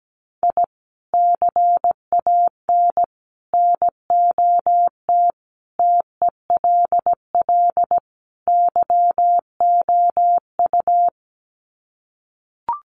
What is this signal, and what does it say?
Morse code, heard as a keyed tone, about 17 wpm, 700 Hz. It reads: I CAN NOT TELL YOU
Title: Sets of 5 Words 17wpm